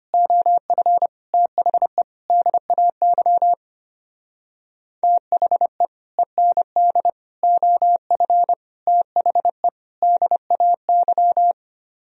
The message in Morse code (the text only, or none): OF THE DAY THE END OF THE DAY